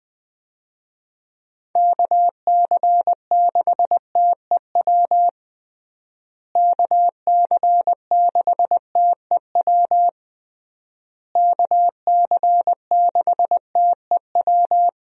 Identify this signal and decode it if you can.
Morse code, heard as a keyed tone, about 21 wpm, 700 Hz. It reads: KC6TEW KC6TEW KC6TEW
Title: USA Call Signs - 6 Characters - Repeated 3x - 20wpm